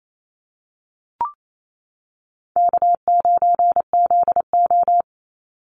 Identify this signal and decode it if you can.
Morse code, heard as a keyed tone, about 28 wpm, 700 Hz. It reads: K9ZO
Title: Easy Call Signs Repeated 2x - 28wpm